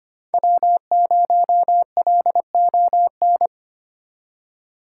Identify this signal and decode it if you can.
Morse code, heard as a keyed tone, about 32 wpm, 700 Hz. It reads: W0LON